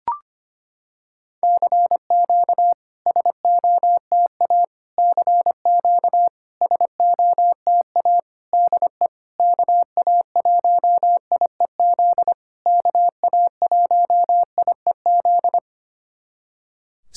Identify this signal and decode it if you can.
Morse code, heard as a keyed tone, about 25 wpm, 700 Hz. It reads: CQ SOTA CQ SOTA DE KA1IEZ KA1IEZ
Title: SOTA QSO Practice 25wpm